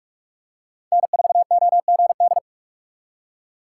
Morse code, heard as a keyed tone, about 42 wpm, 700 Hz. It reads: N4OGD
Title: USA Call Signs - 5 Characters - 6x Character Spacing - 45wpm